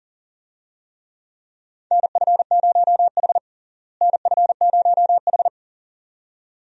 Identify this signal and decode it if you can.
Morse code, heard as a keyed tone, about 40 wpm, 700 Hz. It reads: NF0H NF0H
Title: USA Call Signs - 4 Characters - Repeated 2x - 40wpm